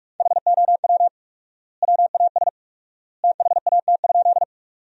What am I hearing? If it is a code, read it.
Morse code, heard as a keyed tone, about 45 wpm, 700 Hz. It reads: HOW WAS THAT?
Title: 4x Word-Spacing - Sentences from Top 100 Words - 45wpm